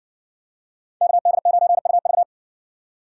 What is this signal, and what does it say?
Morse code, heard as a keyed tone, about 60 wpm, 700 Hz. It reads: KD9UV